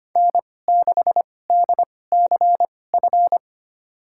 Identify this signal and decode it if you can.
Morse code, heard as a keyed tone, about 25 wpm, spaced as for 18 wpm, 700 Hz. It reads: N6DCF
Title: USA Call Signs - 5 Characters - 2x Character Spacing - 25wpm